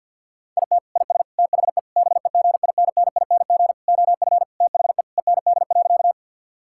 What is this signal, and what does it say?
Morse code, heard as a keyed tone, about 50 wpm, 700 Hz. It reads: IT IS THE BEGINNING OF THE END.
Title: Sentences from Top 1000 Words - Review Part 5 - 50wpm